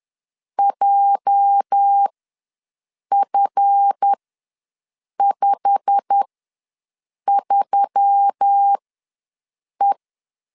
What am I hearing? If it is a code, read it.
Morse code, heard as a keyed tone, about 10 wpm, spaced as for 6 wpm, 800 Hz. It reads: JF53E
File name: LEZIONI / 8 YJ45 / 8 YJ45 ws 10wpm 3843